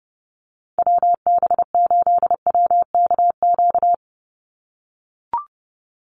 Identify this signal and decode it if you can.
Morse code, heard as a keyed tone, about 30 wpm, 700 Hz. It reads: WB8WKQ
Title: Call Signs - 30wpm